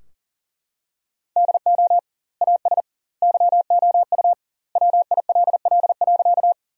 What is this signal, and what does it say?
Morse code, heard as a keyed tone, about 40 wpm, 700 Hz. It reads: DO AS YOU WILL.
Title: 2x Word-Spacing - Sentences from Top 100 Words - 40wpm